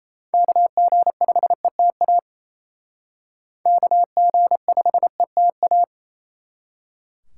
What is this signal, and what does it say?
Morse code, heard as a keyed tone, about 34 wpm, 700 Hz. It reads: KG5ETA KG5ETA
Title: Speed Racing - USA Call Signs 28wpm